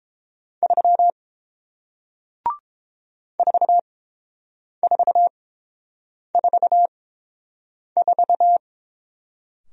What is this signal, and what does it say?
Morse code, heard as a keyed tone, about 31 wpm, spaced as for 7 wpm, 700 Hz. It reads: 3 4444